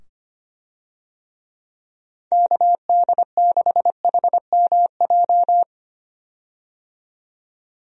Morse code, heard as a keyed tone, about 25 wpm, 700 Hz. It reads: KD6HMJ
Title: USA Call Signs - 6 Characters - 25wpm